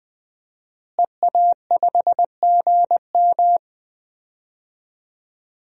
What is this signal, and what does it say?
Morse code, heard as a keyed tone, about 20 wpm, 700 Hz. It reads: EA5GM
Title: DXCC-100 Call Signs - 5 Characters - Repeated 2x - 20wpm